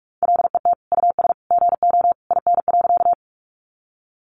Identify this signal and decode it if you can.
Morse code, heard as a keyed tone, about 44 wpm, 700 Hz. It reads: LET US GO IN.
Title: Sentences from Top 300 Words 45wpm